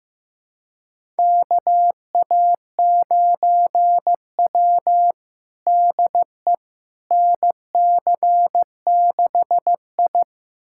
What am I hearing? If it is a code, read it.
Morse code, heard as a keyed tone, about 15 wpm, 700 Hz. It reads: KA9W DE NC6I
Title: USA Callsign DE Callsign - 4 Characters - 15wpm